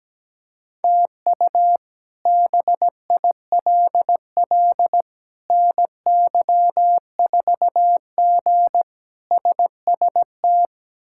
Morse code, heard as a keyed tone, about 17 wpm, 700 Hz. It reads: TU BILL NY4G SST
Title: K1USN SST QSO Practice - 17wpm